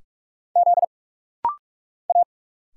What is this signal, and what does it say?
Morse code, heard as a keyed tone, about 44 wpm, 700 Hz. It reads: Z A
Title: Single Letters - Rapid-Fire 45wpm